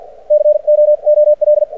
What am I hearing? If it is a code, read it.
Morse code, heard as a keyed tone, about 53 wpm, spaced as for 44 wpm, 600 Hz. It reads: COOL
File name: archived / CRSnet